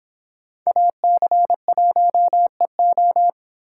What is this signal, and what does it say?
Morse code, heard as a keyed tone, about 29 wpm, 700 Hz. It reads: AC1EO